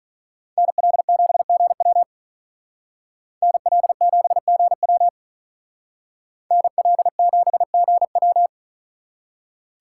Morse code, heard as a keyed tone, about 41 wpm, 700 Hz. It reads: NL7GW NL7GW NL7GW
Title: Speed Racing - USA 2x2 Call Signs 35wpm